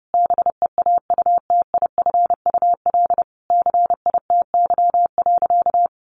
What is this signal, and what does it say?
Morse code, heard as a keyed tone, about 30 wpm, 700 Hz. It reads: BEAUTIFUL CITY.
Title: Sentences from Top 1100 Words 30wpm